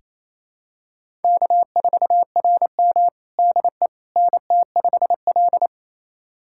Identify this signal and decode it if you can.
Morse code, heard as a keyed tone, about 28 wpm, 700 Hz. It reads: K4RM DE NT5L
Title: USA Callsign DE Callsign - 4 Characters - 28wpm